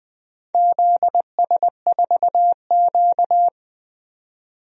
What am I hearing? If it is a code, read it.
Morse code, heard as a keyed tone, about 20 wpm, 700 Hz. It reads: ZS4Q
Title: DXCC-100 Call Signs - 4 Characters - 20wpm